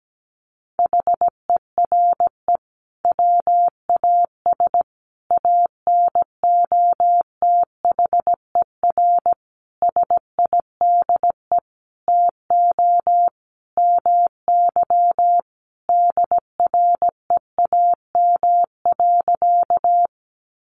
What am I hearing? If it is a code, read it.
Morse code, heard as a keyed tone, about 17 wpm, 700 Hz. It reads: HERE WAS ANOTHER SIDE TO MY DREAM.
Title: Sentences from Top 1500 Words 17wpm